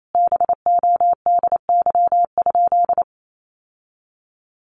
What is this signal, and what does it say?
Morse code, heard as a keyed tone, about 28 wpm, 700 Hz. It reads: BODY?
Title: Sentences from Top 1200 Words 28wpm